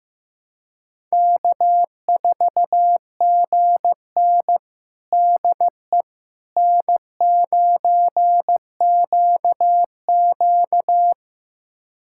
Morse code, heard as a keyed tone, about 15 wpm, 700 Hz. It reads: K4GN DE N9QQ
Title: USA Callsign DE Callsign - 4 Characters - 15wpm